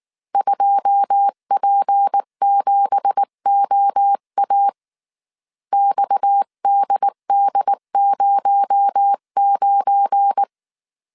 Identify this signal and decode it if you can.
Morse code, heard as a keyed tone, about 19 wpm, 800 Hz. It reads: 2P7OA XDD09
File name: CW / DET / 1 9 1500C S B 18wpm 2551